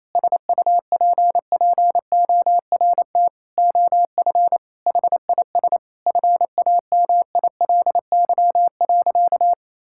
Morse code, heard as a keyed tone, about 28 wpm, 700 Hz. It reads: SUPPORT OF HIS FAMILY.
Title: Sentences from Top 1000 Words - Review Part 5 - 28wpm